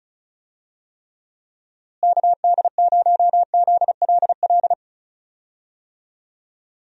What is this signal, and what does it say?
Morse code, heard as a keyed tone, about 36 wpm, 700 Hz. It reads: KD0ZLL
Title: USA Call Signs - 6 Characters - Repeated 2x - 35wpm